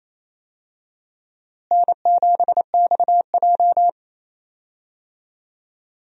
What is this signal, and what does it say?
Morse code, heard as a keyed tone, about 28 wpm, 700 Hz. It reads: N7XJ
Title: USA Call Signs - 4 Characters - Repeated 3x - 28wpm